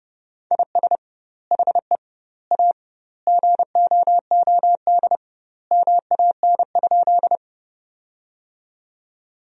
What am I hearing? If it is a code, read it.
Morse code, heard as a keyed tone, about 29 wpm, 700 Hz. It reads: IS HE A GOOD MAN?
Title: 2x Word-Spacing - Sentences from Top 200 Words - 30wpm